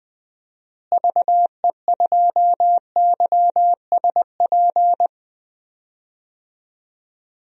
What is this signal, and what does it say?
Morse code, heard as a keyed tone, about 21 wpm, 700 Hz. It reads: VE2YSP
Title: DXCC-100 Call Signs - 6 Characters - Repeated 2x - 20wpm